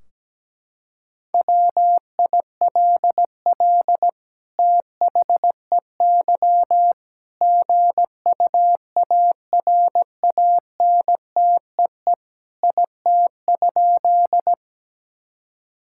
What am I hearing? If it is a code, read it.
Morse code, heard as a keyed tone, about 17 wpm, 700 Hz. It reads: WILL THEY GUARANTEE IT?
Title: Sentences from Top 1600 Words 17wpm